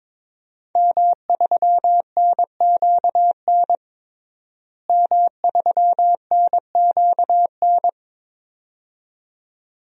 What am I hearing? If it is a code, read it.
Morse code, heard as a keyed tone, about 22 wpm, 700 Hz. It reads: M3NQN M3NQN
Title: DXCC-100 Call Signs - 5 Characters - Repeated 2x - 22wpm